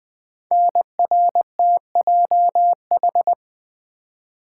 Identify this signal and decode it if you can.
Morse code, heard as a keyed tone, about 20 wpm, 700 Hz. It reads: NRTJH
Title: Code Groups - 5 Letters - 20wpm